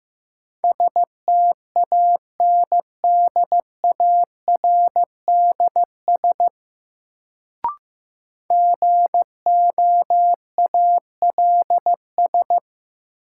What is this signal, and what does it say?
Morse code, heard as a keyed tone, about 15 wpm, 700 Hz. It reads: STANDARDS GOALS
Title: Sentences from Top 1000 Words - Review Part 6 - New Word Drill - 15wpm